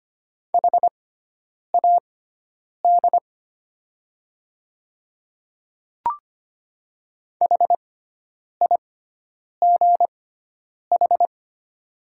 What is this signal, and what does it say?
Morse code, heard as a keyed tone, about 25 wpm, spaced as for 9 wpm, 700 Hz. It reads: HAD HIGH
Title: 6x Character Spacing - Top 300 Words - 25wpm